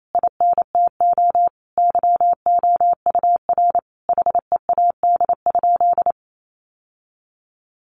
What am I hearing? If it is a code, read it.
Morse code, heard as a keyed tone, about 28 wpm, 700 Hz. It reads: INTO YOUR HEAD?
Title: Sentences from Top 600 Words 28wpm